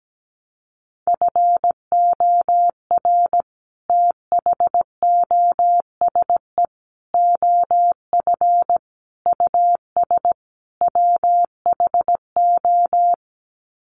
Morse code, heard as a keyed tone, about 17 wpm, 700 Hz. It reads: FOR THOSE OF US WHO
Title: Sets of 5 Words Encore 17wpm